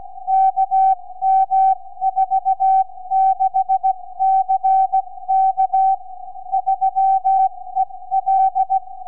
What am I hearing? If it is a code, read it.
Morse code, heard as a keyed tone, about 16 wpm, spaced as for 14 wpm, 770 Hz. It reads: KM46CK 3EL
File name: sv5four 4m